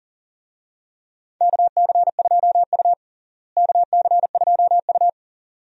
Morse code, heard as a keyed tone, about 40 wpm, 700 Hz. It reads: KC2U KC2U
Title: USA Call Signs - 4 Characters - Repeated 2x - 40wpm